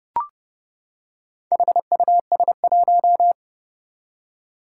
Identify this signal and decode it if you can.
Morse code, heard as a keyed tone, about 30 wpm, 700 Hz. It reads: HUS1
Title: Four Challenging Letter-Number Combinations 30wpm